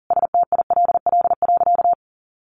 Morse code, heard as a keyed tone, about 40 wpm, 700 Hz. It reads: STILL.
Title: Sentences from Top 1500 Words 40wpm